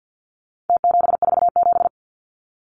Morse code, heard as a keyed tone, about 49 wpm, 700 Hz. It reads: T747
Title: Numbers - Contest Style - 4-Digit 50wpm